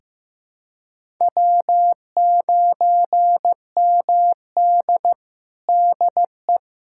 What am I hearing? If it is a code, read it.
Morse code, heard as a keyed tone, about 15 wpm, 700 Hz. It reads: W9MD DE
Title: USA Callsign DE Callsign - 4 Characters - 15wpm